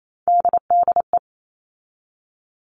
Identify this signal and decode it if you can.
Morse code, heard as a keyed tone, about 28 wpm, 700 Hz. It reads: DDE
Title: All Cut Numbers - 3-Digit 28wpm